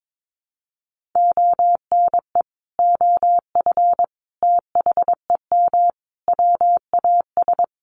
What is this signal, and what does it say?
Morse code, heard as a keyed tone, about 22 wpm, 700 Hz. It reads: ONE OF THEM WAS
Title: Sentences from Top 1500 Words 22wpm